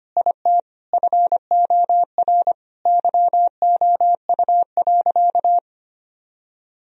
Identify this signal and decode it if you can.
Morse code, heard as a keyed tone, about 25 wpm, 700 Hz. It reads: IT FOR YOU.
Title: Sentences from Top 1000 Words - Review Part 2 - 25wpm